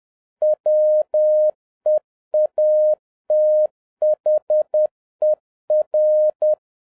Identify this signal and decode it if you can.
Morse code, heard as a keyed tone, about 10 wpm, 610 Hz. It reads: WEATHER